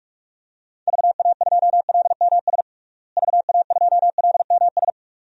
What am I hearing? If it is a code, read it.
Morse code, heard as a keyed tone, about 46 wpm, 700 Hz. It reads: VA2LMS VA2LMS
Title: DXCC-100 Call Signs - 6 Characters - Repeated 2x - 45wpm